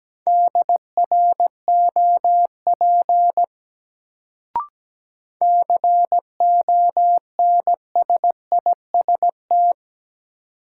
Morse code, heard as a keyed tone, about 17 wpm, 700 Hz. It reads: DROP CONSIST
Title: Sentences from Top 1400 Words - New Word Drill - 17wpm